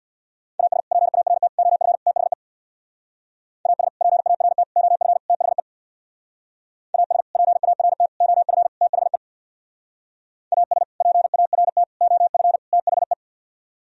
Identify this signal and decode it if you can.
Morse code, heard as a keyed tone, about 59 wpm, 700 Hz. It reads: AS PART OF THE AS PART OF THE AS PART OF THE AS PART OF THE
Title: Speed-Racing - Sets of 4 Words 50wpm